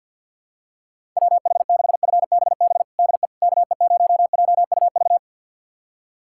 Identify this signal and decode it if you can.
Morse code, heard as a keyed tone, about 52 wpm, 700 Hz. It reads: WH6FBB DE KE0JUV
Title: USA Callsign DE Callsign 50wpm